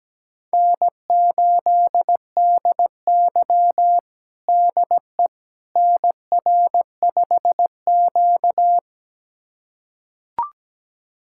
Morse code, heard as a keyed tone, about 17 wpm, 700 Hz. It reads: N8DY DE NR5Q
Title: USA Callsign DE Callsign - 4 Characters - 17wpm